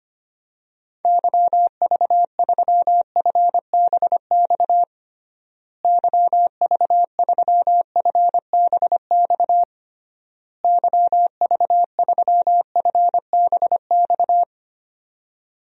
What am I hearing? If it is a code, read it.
Morse code, heard as a keyed tone, about 25 wpm, 700 Hz. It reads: YV3FBX YV3FBX YV3FBX
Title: DXCC-100 Call Signs - 6 Characters - Repeated 3x - 25wpm